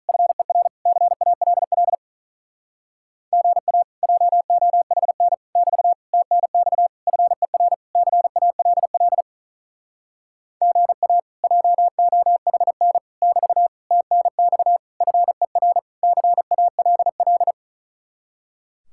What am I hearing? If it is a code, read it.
Morse code, heard as a keyed tone, about 47 wpm, 700 Hz. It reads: FER CALL GA JOHN <BT> TNX FER CALL GA JOHN <BT> TNX FER CALL